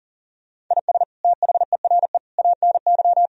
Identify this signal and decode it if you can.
Morse code, heard as a keyed tone, about 40 wpm, 700 Hz. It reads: IS THERE ANY
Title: Sentences from Top 1000 Words - Review Part 4 - 40wpm